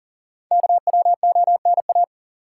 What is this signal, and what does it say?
Morse code, heard as a keyed tone, about 40 wpm, 700 Hz. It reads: KWONA
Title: Code Groups - 5 Letters - 40wpm